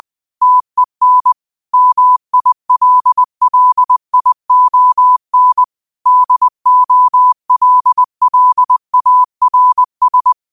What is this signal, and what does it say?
Morse code, heard as a keyed tone, about 20 wpm, 1000 Hz. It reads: TEN MILLION DOLLARS
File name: qotdfiles / 20wpm